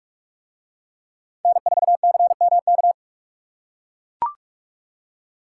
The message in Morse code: N4CMK

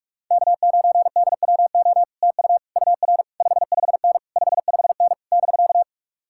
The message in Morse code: K9DWO TU UR 55N 55N <BK>

